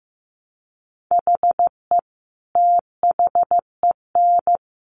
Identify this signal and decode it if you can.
Morse code, heard as a keyed tone, about 14 wpm, 700 Hz. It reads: HE THEN
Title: Sentences from Top 300 Words 15wpm